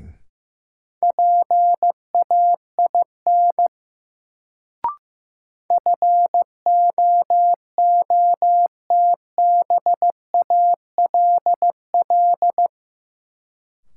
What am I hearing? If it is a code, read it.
Morse code, heard as a keyed tone, about 15 wpm, 700 Hz. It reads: PAIN FOOTBALL
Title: Sentences from Top 1600 Words - New Word Drill Alt - 15wpm